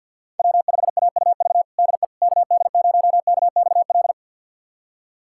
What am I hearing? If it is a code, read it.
Morse code, heard as a keyed tone, about 45 wpm, 700 Hz. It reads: W5AUV DE KD0KXL
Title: USA Callsign DE Callsign 50wpm